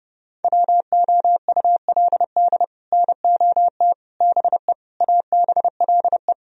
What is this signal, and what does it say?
Morse code, heard as a keyed tone, about 38 wpm, 700 Hz. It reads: WOULD NOT BE ABLE